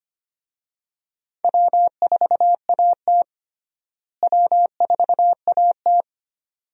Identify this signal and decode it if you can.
Morse code, heard as a keyed tone, about 25 wpm, 700 Hz. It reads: W4AT W4AT
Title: USA Call Signs - 4 Characters - Repeated 2x - 25wpm